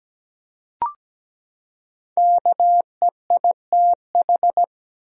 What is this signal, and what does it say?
Morse code, heard as a keyed tone, about 17 wpm, 700 Hz. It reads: KEITH
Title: USA Common Names - Max 5 Letters - 17wpm